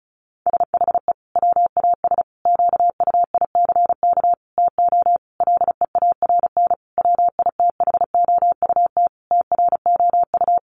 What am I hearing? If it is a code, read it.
Morse code, heard as a keyed tone, about 34 wpm, 700 Hz. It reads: SHE WAS QUICK TO LEARN WITHOUT TROU
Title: Sentences from Top 1200 Words 35wpm